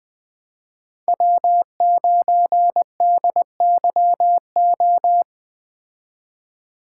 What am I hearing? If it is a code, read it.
Morse code, heard as a keyed tone, about 20 wpm, 700 Hz. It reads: W9DYO
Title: Speed Racing - USA 1x3 Call Signs 17wpm